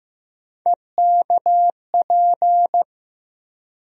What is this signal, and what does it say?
Morse code, heard as a keyed tone, about 15 wpm, 700 Hz. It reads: EKP
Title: USA Call Sign Postfixes - 3 Letters 15wpm